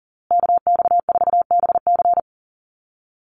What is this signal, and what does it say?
Morse code, heard as a keyed tone, about 40 wpm, 700 Hz. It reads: KX4BC
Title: USA Call Signs Repeated 2x 40wpm